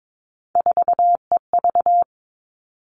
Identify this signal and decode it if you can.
Morse code, heard as a keyed tone, about 22 wpm, 700 Hz. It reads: 4EV
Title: All Cut Numbers - 3-Digit 22wpm